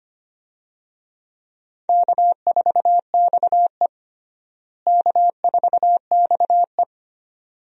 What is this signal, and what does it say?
Morse code, heard as a keyed tone, about 25 wpm, 700 Hz. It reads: K4XE K4XE
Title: USA Call Signs - 4 Characters - Repeated 2x - 25wpm